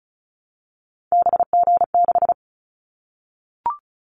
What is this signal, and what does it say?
Morse code, heard as a keyed tone, about 34 wpm, 700 Hz. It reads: BG6
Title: Callsign Prefixes for Top 100 DXCC - 35wpm